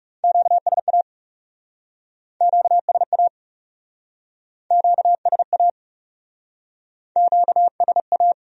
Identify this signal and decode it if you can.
Morse code, heard as a keyed tone, about 40 wpm, 700 Hz. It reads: QSA QSA QSA QSA